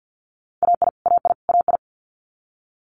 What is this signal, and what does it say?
Morse code, heard as a keyed tone, about 50 wpm, 700 Hz. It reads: AI AI AI
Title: Multi-letter Phonemes Repeated 3x 50wpm